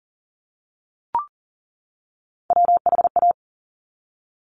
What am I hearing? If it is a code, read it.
Morse code, heard as a keyed tone, about 40 wpm, 700 Hz. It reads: WHA
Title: Top Three Letter Combinations in Words 40wpm